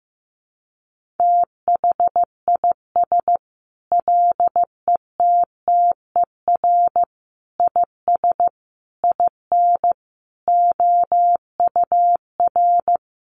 Morse code, heard as a keyed tone, about 15 wpm, 700 Hz. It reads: THIS LETTER IS IN OUR
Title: Sentences from Top 1400 Words 15wpm